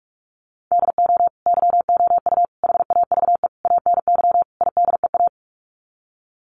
Morse code, heard as a keyed tone, about 44 wpm, 700 Hz. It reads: DO YOU HAVE ANY IDEA
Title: Sets of 5 Words Encore 45wpm